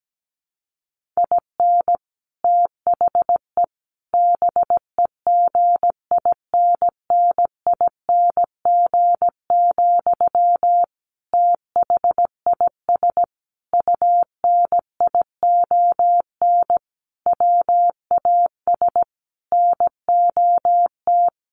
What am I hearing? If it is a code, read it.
Morse code, heard as a keyed tone, about 17 wpm, 700 Hz. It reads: IN THE BEGINNING, THIS UNION WAS NOT
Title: Sentences from Top 1400 Words 17wpm